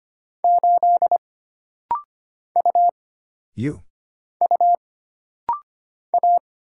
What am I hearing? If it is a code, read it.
Morse code, heard as a keyed tone, about 25 wpm, 700 Hz. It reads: 8 U U A
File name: Single Character (Letter:Number:Punctuation:Prosign) - Rapid-Fire - 25wpm